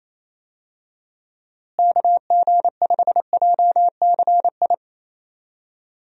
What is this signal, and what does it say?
Morse code, heard as a keyed tone, about 28 wpm, 700 Hz. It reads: KG5JCI
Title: USA Call Signs - 6 Characters - Repeated 3x - 28wpm